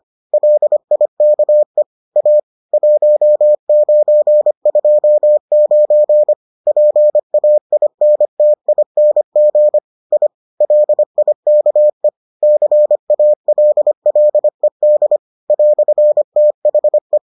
Morse code, heard as a keyed tone, about 25 wpm, 600 Hz. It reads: LIKE A 1929 PAINTING I LIKE CALLED "THE